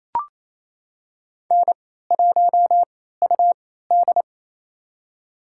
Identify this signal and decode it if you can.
Morse code, heard as a keyed tone, about 28 wpm, spaced as for 16 wpm, 700 Hz. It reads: N1UD
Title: USA Call Signs - 4 Characters - 3x Character Spacing - 28wpm